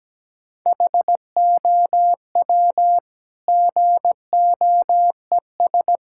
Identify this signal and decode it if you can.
Morse code, heard as a keyed tone, about 17 wpm, 700 Hz. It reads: HOW GOES
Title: Sentences from Top 1000 Words - Review Part 5 - 17wpm